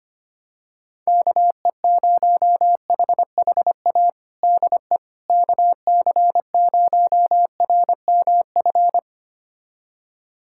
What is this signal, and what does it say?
Morse code, heard as a keyed tone, about 25 wpm, 700 Hz. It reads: KE0HHA DE KC0RMF
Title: USA Callsign DE Callsign 25wpm